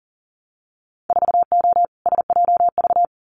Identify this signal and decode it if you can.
Morse code, heard as a keyed tone, about 40 wpm, 700 Hz. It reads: 4O SJV
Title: Winter Field Day Exchanges Encore 40wpm